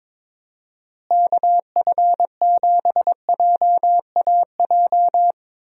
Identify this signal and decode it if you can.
Morse code, heard as a keyed tone, about 22 wpm, 700 Hz. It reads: KF7JAJ
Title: USA Call Signs - 6 Characters - Repeated 2x - 22wpm